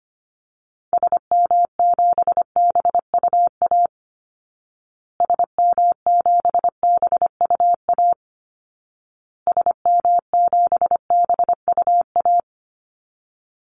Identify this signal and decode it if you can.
Morse code, heard as a keyed tone, about 25 wpm, 700 Hz. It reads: SM7BUA SM7BUA SM7BUA
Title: Call Signs Repeated 2x - 25wpm